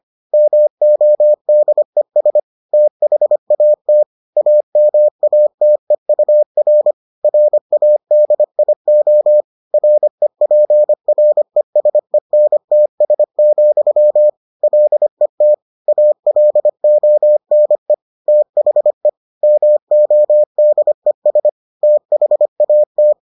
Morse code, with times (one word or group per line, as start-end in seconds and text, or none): 0.3-2.4 MODES
2.7-4.0 THAT
4.4-6.9 AMATEUR
7.2-9.4 RADIO
9.7-14.3 REPRESENTS,
14.6-15.5 LET
15.9-17.9 ALONE
18.3-19.1 THE
19.4-21.5 MODES
21.8-23.1 THAT